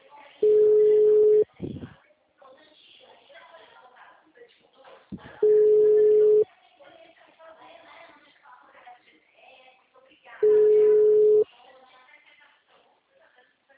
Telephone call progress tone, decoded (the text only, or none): ringback tone